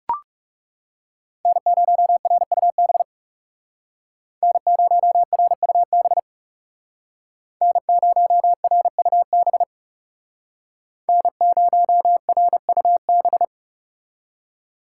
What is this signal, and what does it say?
Morse code, heard as a keyed tone, about 35 wpm, 700 Hz. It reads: N0RUB N0RUB N0RUB N0RUB
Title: Speed Racing - USA Call Signs 30wpm